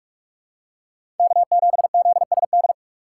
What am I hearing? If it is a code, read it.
Morse code, heard as a keyed tone, about 46 wpm, 700 Hz. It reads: K7ZSD
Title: USA Call Signs - 6x Character Spacing - 45wpm